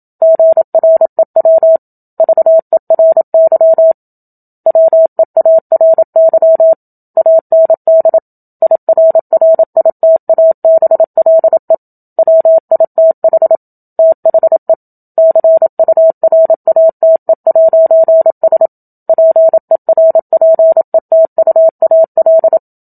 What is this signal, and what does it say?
Morse code, heard as a keyed tone, about 27 wpm, 650 Hz. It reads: GREW VERY WEARY AND IRRITABLE WITH THE CURATE'S PERPETUAL
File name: Book Two / 01-Chapter 1 - 27 WPM